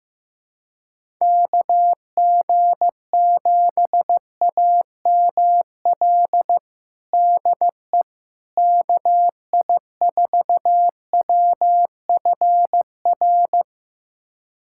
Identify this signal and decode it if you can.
Morse code, heard as a keyed tone, about 15 wpm, 700 Hz. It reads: KG7AML DE KI4WFR
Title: USA Callsign DE Callsign 15wpm